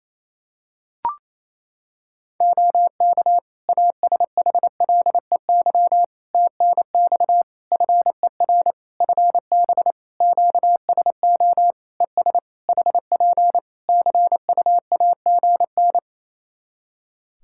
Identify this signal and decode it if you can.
Morse code, heard as a keyed tone, about 28 wpm, 700 Hz. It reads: OK ASHLEY TNX FER FB QSO ES HP CUAGN
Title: Practice QSO - Exchange 4 - Ending - 28wpm